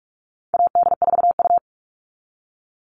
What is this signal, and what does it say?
Morse code, heard as a keyed tone, about 44 wpm, 700 Hz. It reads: AD4U
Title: All Cut Numbers - 4-Digit 45wpm